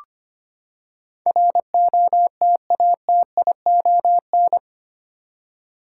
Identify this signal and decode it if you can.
Morse code, heard as a keyed tone, about 25 wpm, 700 Hz. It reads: ROTATION
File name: Eight Letter Words - 25wpm